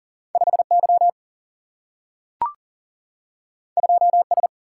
Morse code, heard as a keyed tone, about 40 wpm, 700 Hz. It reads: 5Y 2S
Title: Single-Letter Single-Number 40wpm